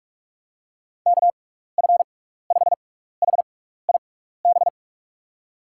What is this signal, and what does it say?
Morse code, heard as a keyed tone, about 44 wpm, spaced as for 15 wpm, 700 Hz. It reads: KF5HIB
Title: USA Call Signs - 6 Characters - 6x Character Spacing - 45wpm